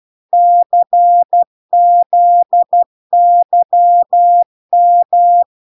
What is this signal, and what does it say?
Morse code, heard as a keyed tone, about 12 wpm, 700 Hz. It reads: CZYM